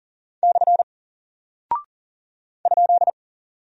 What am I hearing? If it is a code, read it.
Morse code, heard as a keyed tone, about 39 wpm, 700 Hz. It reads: / ?
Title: Single Character (Letter:Number:Punctuation:Prosign) 40wpm